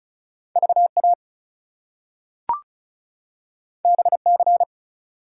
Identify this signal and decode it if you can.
Morse code, heard as a keyed tone, about 36 wpm, 700 Hz. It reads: VA BC
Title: US States and Canadian Provinces 35wpm